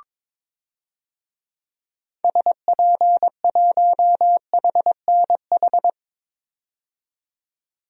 Morse code, heard as a keyed tone, about 22 wpm, 700 Hz. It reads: SP1HNH
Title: DXCC-100 Call Signs - 6 Characters - 22wpm